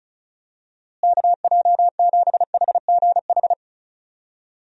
Morse code, heard as a keyed tone, about 36 wpm, 700 Hz. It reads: KJ7HGH
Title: USA Call Signs - 6 Characters - Repeated 2x - 35wpm